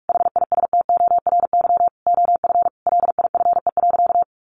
Morse code, heard as a keyed tone, about 44 wpm, 700 Hz. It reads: HISTORY OF LIFE.
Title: A Sound Of Thunder 45wpm